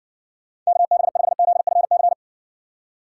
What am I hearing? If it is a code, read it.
Morse code, heard as a keyed tone, about 56 wpm, 700 Hz. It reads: KB5ZFC